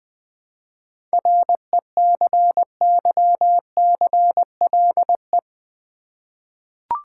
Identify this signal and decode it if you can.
Morse code, heard as a keyed tone, about 20 wpm, 690 Hz. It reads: RECYCLE
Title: 7 Letter Words - First 3 Letters Given - 20wpm